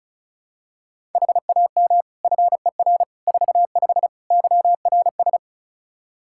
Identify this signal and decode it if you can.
Morse code, heard as a keyed tone, about 34 wpm, 700 Hz. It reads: HAM FER 45 YRS
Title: Practice QSO - Exchange 3 - More Info - 35wpm